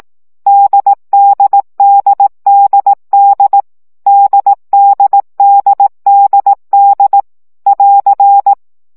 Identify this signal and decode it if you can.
Morse code, heard as a keyed tone, about 18 wpm, 800 Hz. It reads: DDDDD DDDDD <AR>
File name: CW gyakorlas d 15 baud